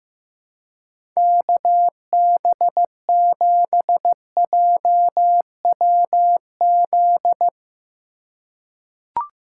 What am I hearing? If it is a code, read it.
Morse code, heard as a keyed tone, about 15 wpm, 700 Hz. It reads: KB7JWZ
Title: USA Call Signs - 6 Characters - 2x Character Spacing - 15wpm